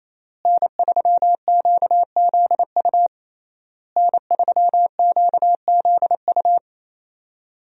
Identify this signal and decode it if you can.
Morse code, heard as a keyed tone, about 28 wpm, 700 Hz. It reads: N3QZU N3QZU
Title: USA Call Signs - 5 Characters - Repeated 3x - 28wpm